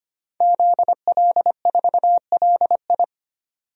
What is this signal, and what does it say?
Morse code, heard as a keyed tone, about 25 wpm, 700 Hz. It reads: ZL4LI
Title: DXCC-100 Call Signs - 5 Characters - 25wpm